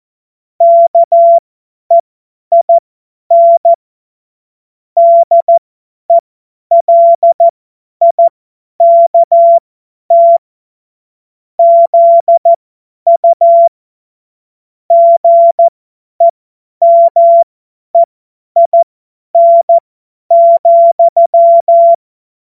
Morse code, heard as a keyed tone, about 14 wpm, spaced as for 10 wpm, 680 Hz. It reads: KEIN DELIKT ZU GEMEIN,
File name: Morsekurs / Witze / Witz 10. Mai 2022 50 F70